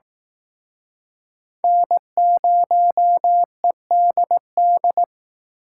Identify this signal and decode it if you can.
Morse code, heard as a keyed tone, about 18 wpm, 700 Hz. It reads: N0EDD
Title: Speed Racing - USA 1x3 Call Signs 15wpm